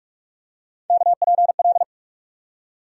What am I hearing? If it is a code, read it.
Morse code, heard as a keyed tone, about 42 wpm, 700 Hz. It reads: KPL
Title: USA Call Sign Postfixes - 3 Letters 45wpm